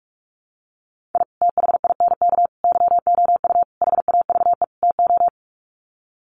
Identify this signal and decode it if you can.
Morse code, heard as a keyed tone, about 44 wpm, 700 Hz. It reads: I THINK YOU HAVE TO
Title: Sets of 5 Words 45wpm